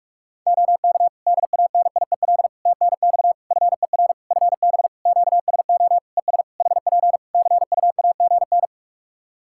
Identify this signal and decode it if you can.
Morse code, heard as a keyed tone, about 44 wpm, 700 Hz. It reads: OK DANIEL TNX FER FB QSO ES HP CUAGN
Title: Practice QSO - Exchange 4 - Ending - 45wpm